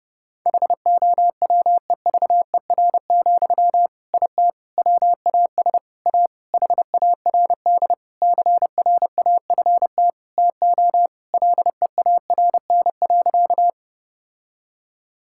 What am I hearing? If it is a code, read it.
Morse code, heard as a keyed tone, about 30 wpm, 700 Hz. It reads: HOWEVER, IT WAS A HARD CRAFT TO LEARN.
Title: Sentences from Top 1000 Words - Review Part 6 - 30wpm